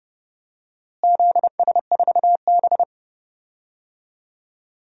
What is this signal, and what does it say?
Morse code, heard as a keyed tone, about 30 wpm, 700 Hz. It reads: ZS4B
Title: DXCC-100 Call Signs - 4 Characters - Repeated 2x - 30wpm